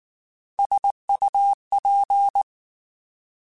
Morse code, heard as a keyed tone, about 19 wpm, 800 Hz. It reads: SUP
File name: Alphabetical / Alphabetical mp3s / Ss @18